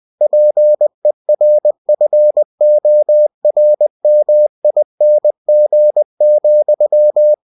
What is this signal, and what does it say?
Morse code, heard as a keyed tone, about 20 wpm, 600 Hz. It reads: PERFORMING,
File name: cw-books / winnie-the-pooh / winnie-20wpm-page-0114